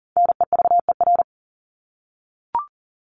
Morse code, heard as a keyed tone, about 40 wpm, 700 Hz. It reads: NEVER
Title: Top 500 Words Repeated 40wpm